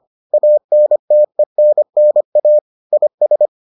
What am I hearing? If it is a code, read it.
Morse code, heard as a keyed tone, about 25 wpm, 600 Hz. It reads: ANTENNA IS